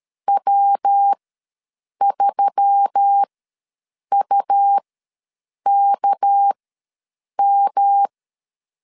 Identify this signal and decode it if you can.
Morse code, heard as a keyed tone, about 13 wpm, spaced as for 7 wpm, 800 Hz. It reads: W3UKM
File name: LEZIONI / 6 PKX0 / 6 PKX ws 12wpm 3106